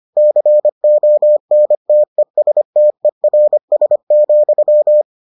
Morse code, heard as a keyed tone, about 25 wpm, 600 Hz. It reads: CONTESTERS,